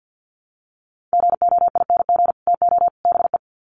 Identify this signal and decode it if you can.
Morse code, heard as a keyed tone, about 50 wpm, 700 Hz. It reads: GOING TO BE
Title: Sets of 3 Words 50wpm